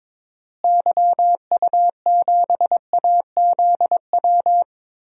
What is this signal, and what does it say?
Morse code, heard as a keyed tone, about 22 wpm, 700 Hz. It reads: YU7AZW
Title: DXCC-100 Call Signs - 6 Characters - Repeated 3x - 22wpm